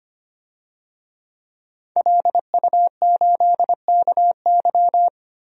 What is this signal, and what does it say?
Morse code, heard as a keyed tone, about 25 wpm, 700 Hz. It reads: LU8KY
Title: DXCC-100 Call Signs - 5 Characters - 25wpm